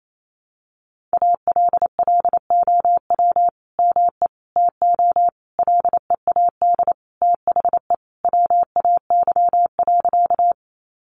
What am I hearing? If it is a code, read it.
Morse code, heard as a keyed tone, about 28 wpm, 700 Hz. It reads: ALLOW ME TO LEAD THE WAY.
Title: Sentences from Top 700 Words 28wpm